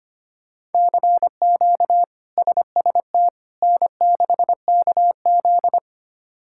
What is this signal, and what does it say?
Morse code, heard as a keyed tone, about 25 wpm, 700 Hz. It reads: CQ SST N6KZ
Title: K1USN SST QSO Practice - 25wpm